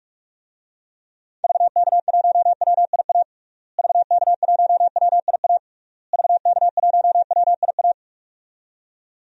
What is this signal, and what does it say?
Morse code, heard as a keyed tone, about 46 wpm, 700 Hz. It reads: VK1WIA VK1WIA VK1WIA
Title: DXCC-100 Call Signs - 6 Characters - Repeated 3x - 45wpm